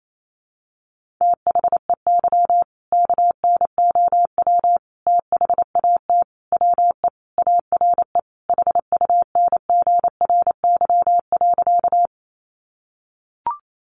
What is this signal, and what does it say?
Morse code, heard as a keyed tone, about 28 wpm, 700 Hz. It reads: THEY KNOW THAT WE ARE HUNGRY.
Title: Sentences from Top 1400 Words 28wpm